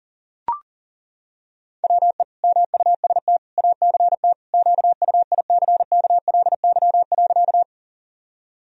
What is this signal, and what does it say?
Morse code, heard as a keyed tone, about 40 wpm, 700 Hz. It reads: WE MUST ACT QUICKLY.
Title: Sentences from Top 1000 Words - Review Part 1 - 40wpm